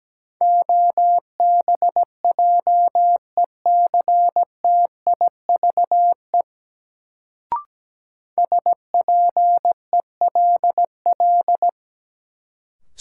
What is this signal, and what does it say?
Morse code, heard as a keyed tone, about 17 wpm, 700 Hz. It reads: OBJECTIVE SPELL
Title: Sentences from Top 1200 Words - New Word Drill Alt - 17wpm